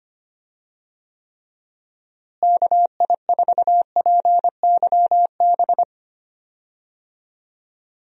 Morse code, heard as a keyed tone, about 25 wpm, 700 Hz. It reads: KI4PYB
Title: USA Call Signs - 6 Characters - Repeated 3x - 25wpm